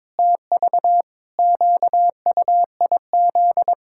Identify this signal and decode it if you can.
Morse code, heard as a keyed tone, about 22 wpm, 700 Hz. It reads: TV QUIZ
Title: Pangram Challenge v2 - 22wpm